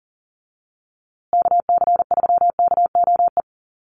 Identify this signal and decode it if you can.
Morse code, heard as a keyed tone, about 40 wpm, 700 Hz. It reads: KC3KOE
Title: USA Call Signs Repeated 3x 40wpm